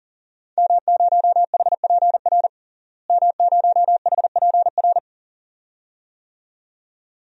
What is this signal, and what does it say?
Morse code, heard as a keyed tone, about 40 wpm, 700 Hz. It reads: M0HPR M0HPR
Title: DXCC-100 Call Signs - 5 Characters - Repeated 2x - 40wpm